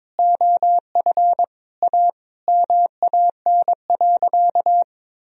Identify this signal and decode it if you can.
Morse code, heard as a keyed tone, about 21 wpm, 700 Hz. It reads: OF A MAN.
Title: Sentences from Top 1000 Words - Review Part 2 - 22wpm